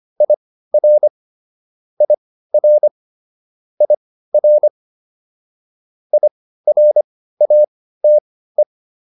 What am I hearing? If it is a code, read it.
Morse code, heard as a keyed tone, about 25 wpm, spaced as for 15 wpm, 600 Hz. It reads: IR IR IR IRATE